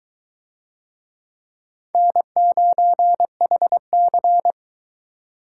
Morse code, heard as a keyed tone, about 22 wpm, 700 Hz. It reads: N9HC